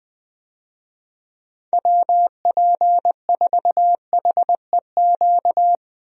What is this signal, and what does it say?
Morse code, heard as a keyed tone, about 20 wpm, 700 Hz. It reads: WP4HEQ
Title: USA Call Signs - 6 Characters - Repeated 2x - 20wpm